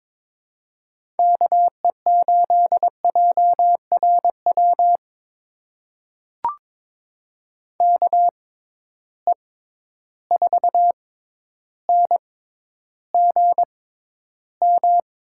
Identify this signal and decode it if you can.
Morse code, heard as a keyed tone, about 22 wpm, spaced as for 11 wpm, 700 Hz. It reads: KE8JRW KE4NGM
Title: USA Call Signs - 6 Characters - 6x Character Spacing - 22wpm